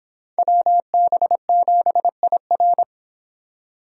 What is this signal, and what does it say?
Morse code, heard as a keyed tone, about 25 wpm, 700 Hz. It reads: WB7IR